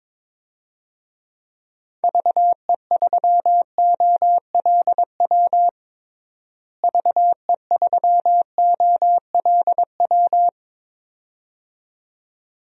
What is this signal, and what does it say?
Morse code, heard as a keyed tone, about 21 wpm, 700 Hz. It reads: VE3OLW VE3OLW
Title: DXCC-100 Call Signs - 6 Characters - Repeated 2x - 22wpm